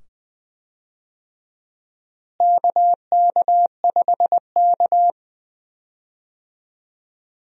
Morse code, heard as a keyed tone, about 20 wpm, 700 Hz. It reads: KK5K
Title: USA Call Signs - 4 Characters - Repeated 2x - 20wpm